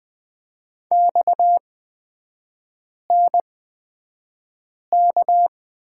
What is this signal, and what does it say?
Morse code, heard as a keyed tone, about 20 wpm, spaced as for 5 wpm, 700 Hz. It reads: XNK
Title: 229 Lesson 37e - Letter x - Three Characters 20wpm